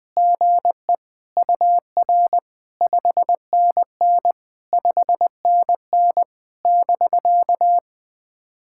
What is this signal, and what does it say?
Morse code, heard as a keyed tone, about 20 wpm, 690 Hz. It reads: GE UR 5NN 5NN <BK>
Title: POTA QSO Practice - 20wpm